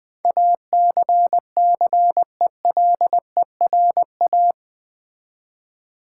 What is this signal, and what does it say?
Morse code, heard as a keyed tone, about 20 wpm, 700 Hz. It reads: ACCELERA
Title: Build A Word - 4 Letters At A Time - 10-Letter Words - 20wpm